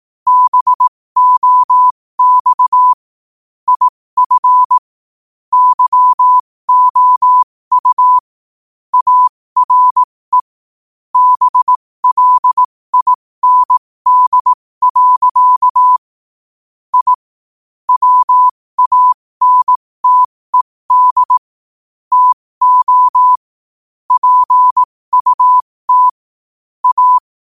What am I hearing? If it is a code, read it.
Morse code, heard as a keyed tone, about 18 wpm, spaced as for 15 wpm, 1000 Hz. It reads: BOX IF YOU ARE BLIND. I WANTED TO PUT A
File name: qotdfiles / 15wpm